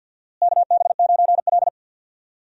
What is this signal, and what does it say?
Morse code, heard as a keyed tone, about 47 wpm, 700 Hz. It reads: KB9L
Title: USA Call Signs - 4 Characters - Repeated 2x - 50wpm